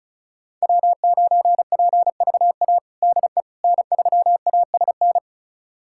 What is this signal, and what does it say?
Morse code, heard as a keyed tone, about 34 wpm, 700 Hz. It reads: W9PVA DE N3ASN
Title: USA Callsign DE Callsign - 5 Characters - 35wpm